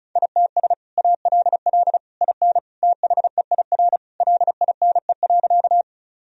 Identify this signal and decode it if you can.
Morse code, heard as a keyed tone, about 36 wpm, 700 Hz. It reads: ITS ALL IN THEIR LINE.
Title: Sentences from Top 1000 Words - Review Part 1 - 35wpm